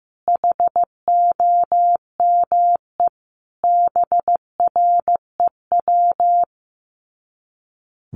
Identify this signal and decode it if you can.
Morse code, heard as a keyed tone, about 15 wpm, 700 Hz. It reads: HOME BREW
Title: Common QSO Phrases 15wpm